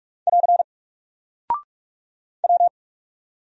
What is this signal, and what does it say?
Morse code, heard as a keyed tone, about 44 wpm, 700 Hz. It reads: <AR> W
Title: Single Character (Letter:Number:Punctuation:Prosign) Alt 45wpm